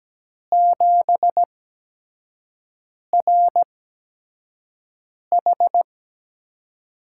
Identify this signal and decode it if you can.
Morse code, heard as a keyed tone, about 17 wpm, spaced as for 5 wpm, 700 Hz. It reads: 7RH
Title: USA Call Signs - 4 Characters - 8x Character Spacing - 17wpm